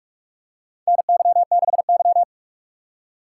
Morse code, heard as a keyed tone, about 46 wpm, 700 Hz. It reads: NY6Y
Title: K1USN SST QSO Practice - 45wpm